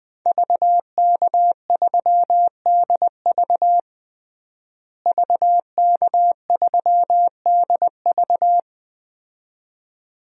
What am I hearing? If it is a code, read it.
Morse code, heard as a keyed tone, about 20 wpm, 700 Hz. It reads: VK3DV VK3DV
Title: DXCC-100 Call Signs - 5 Characters - Repeated 2x - 20wpm